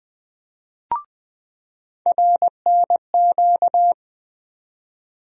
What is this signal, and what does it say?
Morse code, heard as a keyed tone, about 20 wpm, 700 Hz. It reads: RNQ
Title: USA Call Sign Postfixes - 3 Letters 20wpm